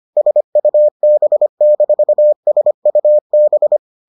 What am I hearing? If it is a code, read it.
Morse code, heard as a keyed tone, about 25 wpm, 600 Hz. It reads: SUB-SUB